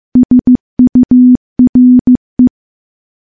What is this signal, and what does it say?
Morse code, heard as a keyed tone, about 15 wpm, 250 Hz. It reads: SURE
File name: morse-training-15WPM